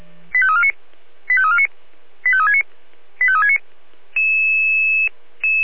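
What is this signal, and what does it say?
Five-tone selective call (ZVEI), heard as quick runs of five tones, tones of about 70 ms. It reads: 85349 85349 85369 85369
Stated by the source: ZVEI five-tone selective call received on an SDR.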